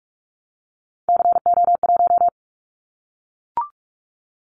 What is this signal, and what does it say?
Morse code, heard as a keyed tone, about 46 wpm, 700 Hz. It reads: CO1
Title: Callsign Prefixes for Top 100 DXCC - 45wpm